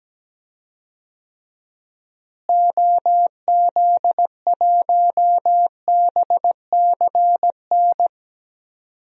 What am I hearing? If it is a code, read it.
Morse code, heard as a keyed tone, about 17 wpm, 700 Hz. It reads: OZ1BCN
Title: DXCC-100 Call Signs - 6 Characters - Repeated 3x - 17wpm